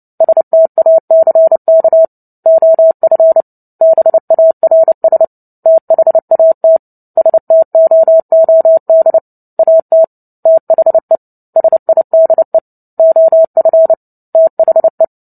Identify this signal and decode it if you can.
Morse code, heard as a keyed tone, about 30 wpm, 650 Hz. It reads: STACK OF BARS THAT STOOD AT THE SIDE OF THE
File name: Book Two / 03-Chapter 3 - 29 WPM